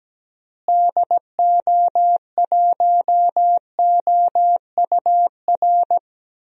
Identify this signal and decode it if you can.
Morse code, heard as a keyed tone, about 17 wpm, 700 Hz. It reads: DO1OUR
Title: DXCC-100 Call Signs - 6 Characters - Repeated 2x - 17wpm